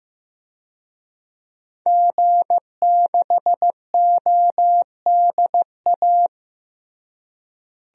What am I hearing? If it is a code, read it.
Morse code, heard as a keyed tone, about 15 wpm, 700 Hz. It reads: G6ODA
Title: DXCC-100 Call Signs - 5 Characters - Repeated 2x - 15wpm